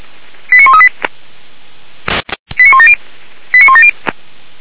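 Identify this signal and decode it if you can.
Five-tone selective call (ZVEI), heard as quick runs of five tones, tones of about 70 ms. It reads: 89138 89170 89170
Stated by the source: ZVEI five-tone selective call received on SDR, address 89131.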